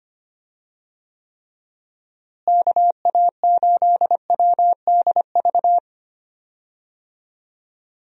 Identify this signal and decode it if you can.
Morse code, heard as a keyed tone, about 25 wpm, 700 Hz. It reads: KA8WDV
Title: USA Call Signs - 6 Characters - Repeated 3x - 25wpm